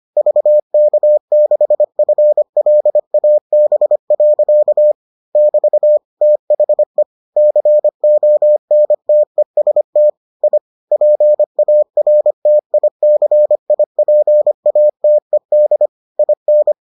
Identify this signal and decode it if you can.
Morse code, heard as a keyed tone, about 25 wpm, 600 Hz. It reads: VK6FLAB. <BT> THE CONTEST I PARTICIPATED IN